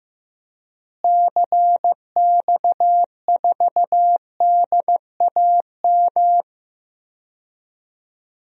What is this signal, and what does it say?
Morse code, heard as a keyed tone, about 15 wpm, 700 Hz. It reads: CX4DAM
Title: DXCC-100 Call Signs - 6 Characters - Repeated 2x - 15wpm